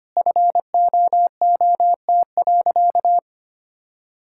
Morse code, heard as a keyed tone, about 25 wpm, 700 Hz. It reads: FOOT.
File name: practice / 25wpm / All-Age Appropriate Jokes - 25wpm